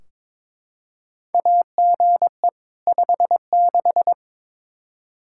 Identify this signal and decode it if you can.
Morse code, heard as a keyed tone, about 22 wpm, 700 Hz. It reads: AGE 56
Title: K1USN SST QSO Practice 22wpm